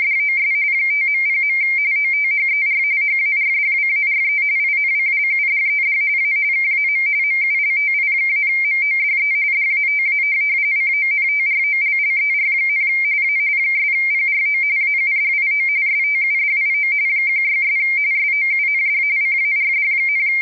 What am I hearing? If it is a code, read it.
RTTY, about 45.45 baud. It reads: DOS 5.0. OTHER HARDWARE INCLUDED A KAM MULTI-MODE DATA CONTROLLER, A 'PRO AUDIO STUDIO 16' SOUND CARD,